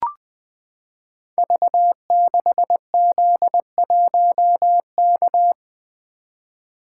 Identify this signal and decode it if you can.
Morse code, heard as a keyed tone, about 20 wpm, 700 Hz. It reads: V6Z1K
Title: Code Groups - 5 Letter:Numbers - 20wpm